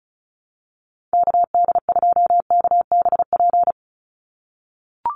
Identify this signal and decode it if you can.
Morse code, heard as a keyed tone, about 34 wpm, 700 Hz. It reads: KD2KBP
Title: USA Call Signs 35wpm